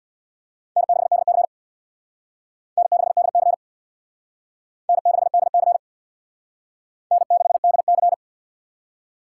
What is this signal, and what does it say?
Morse code, heard as a keyed tone, about 63 wpm, 700 Hz. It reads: N6DC N6DC N6DC N6DC